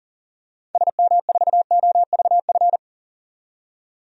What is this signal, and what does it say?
Morse code, heard as a keyed tone, about 40 wpm, 700 Hz. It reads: SM4OVF
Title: DXCC-100 Call Signs - 6 Characters - 40wpm